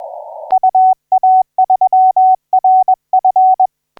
Morse code, heard as a keyed tone, about 21 wpm, 750 Hz. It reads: UA3RF
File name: C91GR 1543UTC 15m